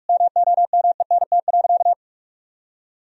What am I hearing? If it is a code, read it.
Morse code, heard as a keyed tone, about 41 wpm, 700 Hz. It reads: MOMENT.
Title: Sentences from Top 1000 Words - Review Part 3 - 45wpm